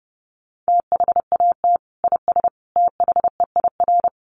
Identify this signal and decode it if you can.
Morse code, heard as a keyed tone, about 30 wpm, 700 Hz. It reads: THAT IS THEIR
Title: Sentences from Top 1400 Words 30wpm